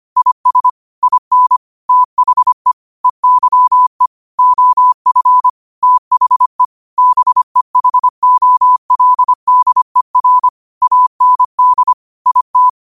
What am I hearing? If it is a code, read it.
Morse code, heard as a keyed tone, about 25 wpm, 1000 Hz. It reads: IS IN THE EYE OF THE BEHOLDER AND IT
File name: qotdfiles / 25wpm